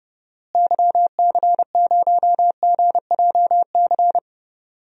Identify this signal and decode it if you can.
Morse code, heard as a keyed tone, about 30 wpm, 700 Hz. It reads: YC0GJC
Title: DXCC-100 Call Signs - 6 Characters - Repeated 3x - 30wpm